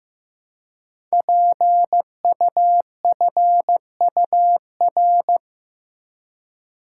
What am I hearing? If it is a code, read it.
Morse code, heard as a keyed tone, about 15 wpm, 700 Hz. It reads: PUFUR
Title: Code Groups - 5 Letters - 15wpm